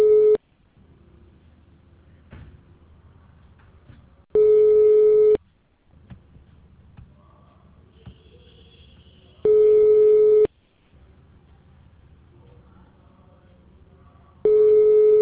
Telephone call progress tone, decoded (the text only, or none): ringback tone